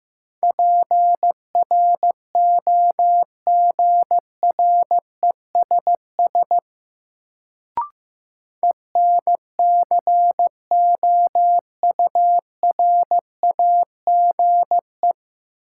Morse code, heard as a keyed tone, about 15 wpm, 700 Hz. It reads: PROGRESS ENCOURAGE
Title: Sentences from Top 1400 Words - New Word Drill Alt - 15wpm